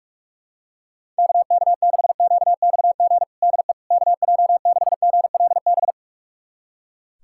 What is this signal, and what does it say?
Morse code, heard as a keyed tone, about 45 wpm, 700 Hz. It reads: KK6QXG DE KJ6GLB
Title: USA Callsign DE Callsign - 6 Characters - 45wpm